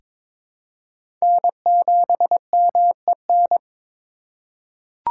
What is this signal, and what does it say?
Morse code, heard as a keyed tone, about 23 wpm, 700 Hz. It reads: N7MEN
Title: USA Call Signs - 4x Character Spacing - 22wpm